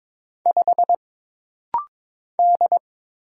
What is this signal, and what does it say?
Morse code, heard as a keyed tone, about 22 wpm, 700 Hz. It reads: 5 D
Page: This rapid-fire timing is geared to push your ICR (Instant Character Recognition) skills.